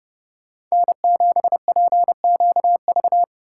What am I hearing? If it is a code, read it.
Morse code, heard as a keyed tone, about 40 wpm, 700 Hz. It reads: N7PQV